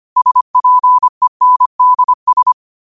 Morse code, heard as a keyed tone, about 25 wpm, 1000 Hz. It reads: SPENDS
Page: New quotes every day in morse code at 25 Words per minute.